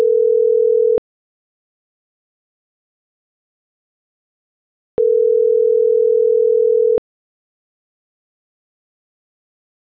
Ringback tone (North America), 440 + 480 Hz.